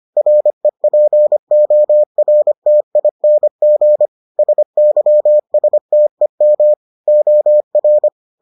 Morse code, heard as a keyed tone, about 25 wpm, 600 Hz. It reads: REPORTING SYSTEM OR